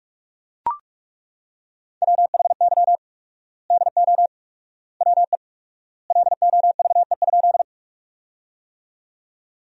Morse code, heard as a keyed tone, about 43 wpm, 700 Hz. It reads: WHY DO WE LOVE?
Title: 4x Word-Spacing - Sentences from Top 400 Words - 45wpm